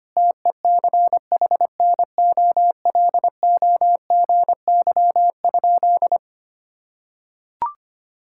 Morse code, heard as a keyed tone, about 25 wpm, 700 Hz. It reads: TECHNOLOGY?
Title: Sentences from Top 1000 Words - Review Part 5 - 25wpm